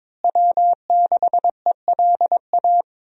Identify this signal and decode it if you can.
Morse code, heard as a keyed tone, about 21 wpm, 700 Hz. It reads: W6ELA
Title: USA Call Signs - 5 Characters - Repeated 3x - 22wpm